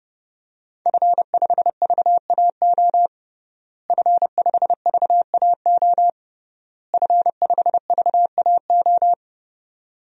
Morse code, heard as a keyed tone, about 32 wpm, 700 Hz. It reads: F5VAO F5VAO F5VAO
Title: DXCC-100 Call Signs - 5 Characters - Repeated 3x - 30wpm